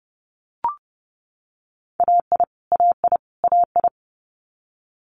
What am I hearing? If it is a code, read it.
Morse code, heard as a keyed tone, about 32 wpm, 700 Hz. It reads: AI AI AI
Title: Multi-letter Phonemes Repeated 3x 30wpm